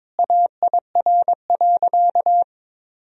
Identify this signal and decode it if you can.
Morse code, heard as a keyed tone, about 21 wpm, 700 Hz. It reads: AIR.
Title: Sentences from Top 1000 Words - Review Part 1 - 22wpm